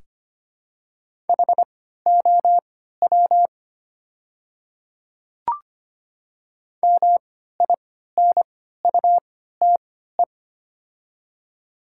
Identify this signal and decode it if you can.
Morse code, heard as a keyed tone, about 25 wpm, spaced as for 14 wpm, 700 Hz. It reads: HOW MINUTE
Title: 3x Character Spacing - Top 500 Words - 25wpm